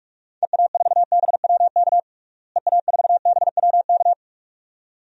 Morse code, heard as a keyed tone, about 46 wpm, 700 Hz. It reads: EA4BWK EA4BWK
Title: DXCC-100 Call Signs - 6 Characters - Repeated 2x - 45wpm